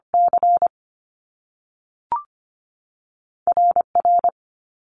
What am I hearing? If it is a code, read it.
Morse code, heard as a keyed tone, about 25 wpm, 700 Hz. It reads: C R R
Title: Top Two Letter Combinations in Words 25wpm